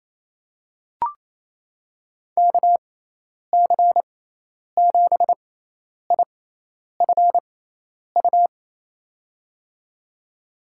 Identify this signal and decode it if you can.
Morse code, heard as a keyed tone, about 28 wpm, spaced as for 10 wpm, 700 Hz. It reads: KC7IFU
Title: USA Call Signs - 6 Characters - 6x Character Spacing - 28wpm